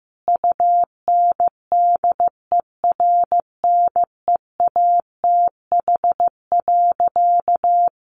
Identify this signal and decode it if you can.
Morse code, heard as a keyed tone, about 15 wpm, 700 Hz. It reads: UNDERNEATH.
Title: A Sound Of Thunder 15wpm